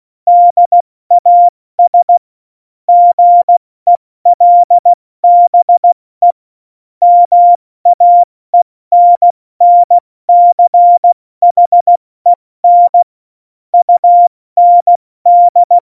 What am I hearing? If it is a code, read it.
Morse code, heard as a keyed tone, about 16 wpm, spaced as for 14 wpm, 700 Hz. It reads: DAS GELBE MAENNCHEN UND
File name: Morsekurs / Witze / Witz 28. November 2017, 70 F80